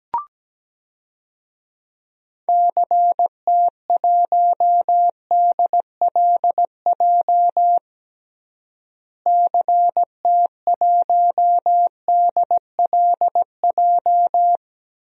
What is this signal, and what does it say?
Morse code, heard as a keyed tone, about 17 wpm, 700 Hz. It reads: CT1DLJ CT1DLJ
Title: DXCC-100 Call Signs - 6 Characters - Repeated 3x - 17wpm